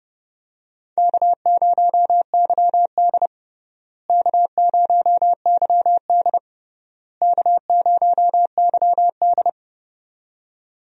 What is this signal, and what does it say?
Morse code, heard as a keyed tone, about 29 wpm, 700 Hz. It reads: K0YD K0YD K0YD
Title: USA Call Signs - 4 Characters - Repeated 3x - 30wpm